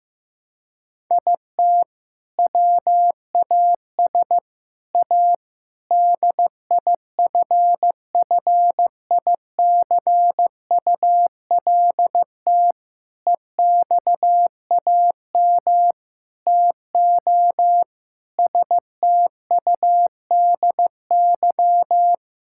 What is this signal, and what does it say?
Morse code, heard as a keyed tone, about 15 wpm, 700 Hz. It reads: IT WAS A DIFFICULT EXAM TO STUDY
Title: Sentences from Top 1000 Words - Review Part 5 - 15wpm